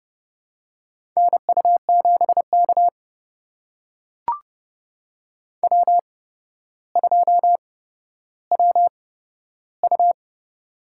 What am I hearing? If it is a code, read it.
Morse code, heard as a keyed tone, about 30 wpm, spaced as for 13 wpm, 700 Hz. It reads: NU7K W2WU
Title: USA Call Signs - 4 Characters - 8x Character Spacing - 30wpm